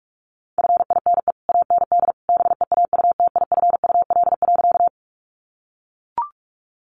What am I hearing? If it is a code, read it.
Morse code, heard as a keyed tone, about 44 wpm, 700 Hz. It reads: FINE AND BEAUTIFUL.
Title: Sentences from Top 900 Words 45wpm